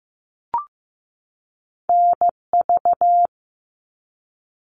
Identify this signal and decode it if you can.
Morse code, heard as a keyed tone, about 15 wpm, 700 Hz. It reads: NV
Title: ARRL Sections 15wpm